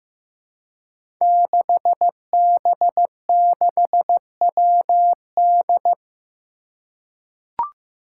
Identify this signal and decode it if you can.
Morse code, heard as a keyed tone, about 15 wpm, 700 Hz. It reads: 6B6WD
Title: Five Challenging Letter-Number Combinations 15wpm